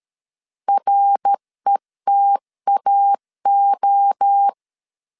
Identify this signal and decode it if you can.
Morse code, heard as a keyed tone, about 13 wpm, 800 Hz. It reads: RETAO
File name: CW / DET / 1 6 1500C S C 12wpm 3533